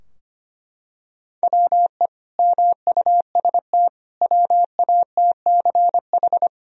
Morse code, heard as a keyed tone, about 25 wpm, 700 Hz. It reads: WE MUST WATCH
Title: Sentences from Top 1000 Words - Review Part 3 - 25wpm